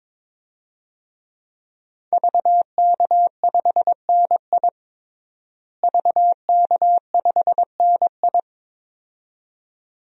Morse code, heard as a keyed tone, about 22 wpm, 700 Hz. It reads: VK5NI VK5NI
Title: DXCC-100 Call Signs - 5 Characters - Repeated 2x - 22wpm